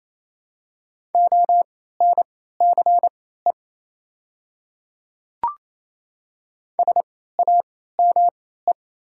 Morse code, heard as a keyed tone, about 28 wpm, spaced as for 16 wpm, 700 Hz. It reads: ONCE SAME
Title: 3x Character Spacing - Top 500 Words - 28wpm